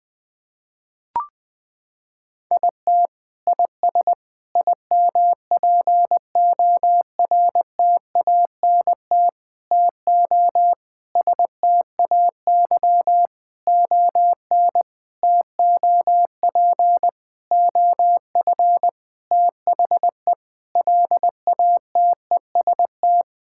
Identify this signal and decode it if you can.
Morse code, heard as a keyed tone, about 20 wpm, 700 Hz. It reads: IT IS IMPORTANT TO STAY ON TOP OF THE LATEST
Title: Sentences with 6-Word Phrases - 20wpm